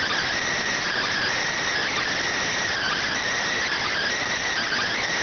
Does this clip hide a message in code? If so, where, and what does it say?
Yes: Emergency Alert System (SAME) tones from 0.8 to 5.0 s, heard as harsh buzzing bursts.